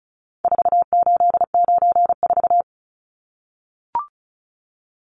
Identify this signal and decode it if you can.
Morse code, heard as a keyed tone, about 34 wpm, 700 Hz. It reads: 4894
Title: Numbers - 4-Digit 35wpm